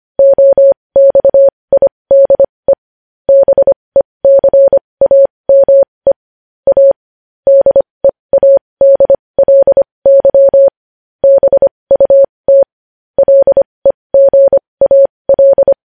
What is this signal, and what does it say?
Morse code, heard as a keyed tone, about 25 wpm, spaced as for 20 wpm, 550 Hz. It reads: OXIDE BECAME A DEADLY BUT LEGAL
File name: podcast / News headlines-1x-25x20